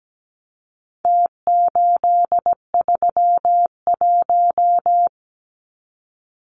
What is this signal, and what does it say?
Morse code, heard as a keyed tone, about 17 wpm, 700 Hz. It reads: T831
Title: Numbers - Contest Style - 4-Digit 17wpm